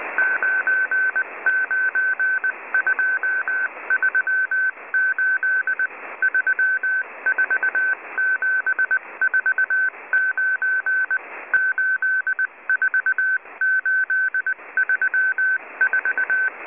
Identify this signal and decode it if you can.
Morse code, heard as a keyed tone, about 19 wpm, 1550 Hz. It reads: 99238 34749 84834